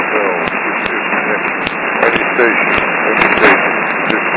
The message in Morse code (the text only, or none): none